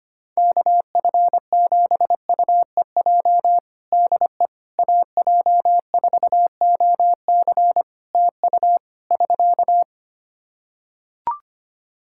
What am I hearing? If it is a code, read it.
Morse code, heard as a keyed tone, about 25 wpm, 700 Hz. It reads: KF7UEJ DE AJ4OC TU <SK>
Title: Practice QSO - Exchange 4 - Ending - 25wpm